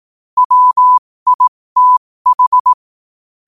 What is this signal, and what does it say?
Morse code, heard as a keyed tone, about 18 wpm, spaced as for 15 wpm, 1000 Hz. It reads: WITH
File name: qotdfiles / 15wpm